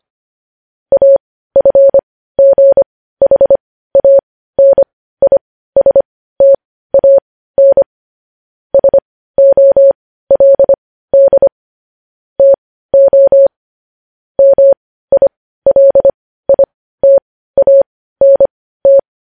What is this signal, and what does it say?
Morse code, heard as a keyed tone, about 25 wpm, spaced as for 15 wpm, 550 Hz. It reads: AFGHANISTAN SOLD TO MILITANT
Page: news-headlines-in-morse-code-at-15-wpm